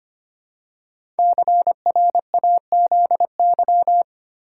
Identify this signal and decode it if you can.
Morse code, heard as a keyed tone, about 25 wpm, 700 Hz. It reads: CRAZY
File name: 5 Letter Words - First 3 Letters Given - 25wpm